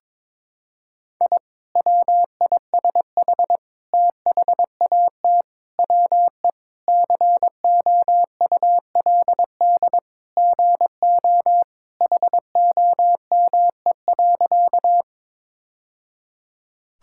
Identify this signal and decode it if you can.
Morse code, heard as a keyed tone, about 21 wpm, 700 Hz. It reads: I WISH THAT WE COULD GO HOME.
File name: practice / 22wpm / Sentences with 2-Word Phrases - 22wpm